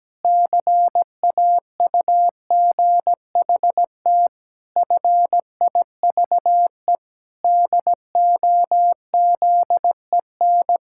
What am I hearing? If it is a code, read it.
Morse code, heard as a keyed tone, about 17 wpm, 700 Hz. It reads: CAUGHT FIVE DOZEN
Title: Pangram Challenge v2 - 17wpm